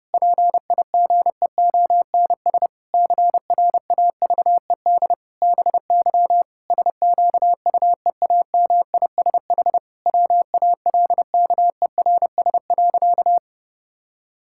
Morse code, heard as a keyed tone, about 30 wpm, 700 Hz. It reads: PIGEONS CRAVED BY SQUEAMISH WALKERS.
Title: Pangram Challenge v2 - 30wpm